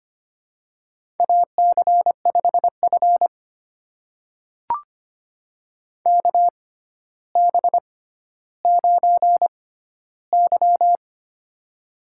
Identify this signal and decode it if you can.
Morse code, heard as a keyed tone, about 25 wpm, spaced as for 13 wpm, 700 Hz. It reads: AC5F KB9Y
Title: USA Call Signs - 4 Characters - 6x Character Spacing - 25wpm